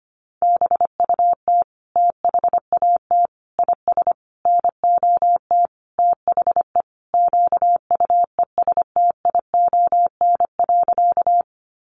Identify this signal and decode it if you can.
Morse code, heard as a keyed tone, about 25 wpm, 700 Hz. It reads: BUT THAT IS NOT THE QUESTION.
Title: Sentences from Top 600 Words 25wpm